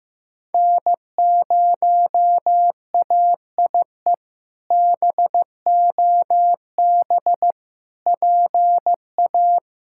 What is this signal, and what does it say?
Morse code, heard as a keyed tone, about 15 wpm, 700 Hz. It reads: N0AIE BOB PA
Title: K1USN SST QSO Practice - 15wpm